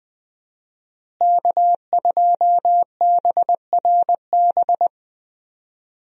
Morse code, heard as a keyed tone, about 20 wpm, 700 Hz. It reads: K2BRB
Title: USA Call Signs - 5 Characters - Repeated 2x - 20wpm